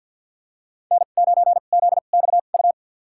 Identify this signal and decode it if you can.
Morse code, heard as a keyed tone, about 46 wpm, spaced as for 35 wpm, 700 Hz. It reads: N9ZXU
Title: USA Call Signs - 2x Character Spacing - 50wpm